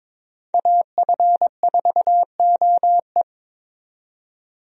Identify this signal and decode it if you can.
Morse code, heard as a keyed tone, about 21 wpm, 700 Hz. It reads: AF4OE
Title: USA Call Signs - 5 Characters - 22wpm